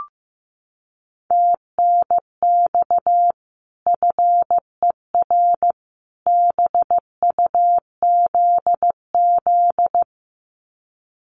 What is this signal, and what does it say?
Morse code, heard as a keyed tone, about 14 wpm, 700 Hz. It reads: TNX FER BUZZ
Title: Common QSO Phrases 15wpm